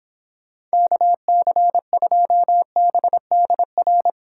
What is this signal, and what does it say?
Morse code, heard as a keyed tone, about 26 wpm, 700 Hz. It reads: KC2BDR